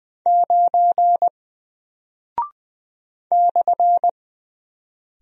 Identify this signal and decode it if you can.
Morse code, heard as a keyed tone, about 20 wpm, 700 Hz. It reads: 9 /
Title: Single Character (Letter:Number:Punctuation:Prosign) Alt 20wpm